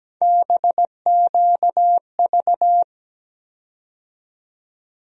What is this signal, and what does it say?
Morse code, heard as a keyed tone, about 17 wpm, 700 Hz. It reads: BQV
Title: USA Call Sign Postfixes - 3 Letters 17wpm